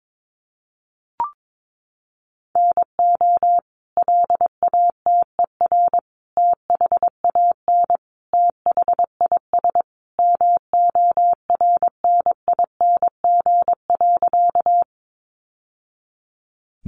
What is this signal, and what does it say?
Morse code, heard as a keyed tone, about 22 wpm, 700 Hz. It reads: NO LATER THAN THIS MORNING.
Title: Sentences from Top 700 Words 22wpm